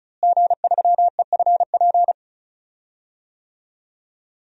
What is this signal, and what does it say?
Morse code, heard as a keyed tone, about 36 wpm, 700 Hz. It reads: G3EFP
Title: DXCC-100 Call Signs - 5 Characters - Repeated 3x - 35wpm